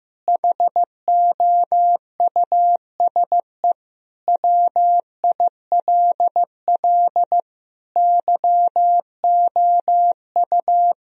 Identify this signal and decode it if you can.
Morse code, heard as a keyed tone, about 15 wpm, 700 Hz. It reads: HOUSE WILL YOU
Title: Sentences from Top 1000 Words - Review Part 1 - 15wpm